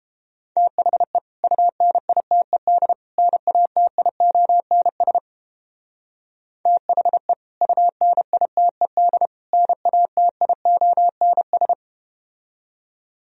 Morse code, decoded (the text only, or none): THE UNITED NATIONS THE UNITED NATIONS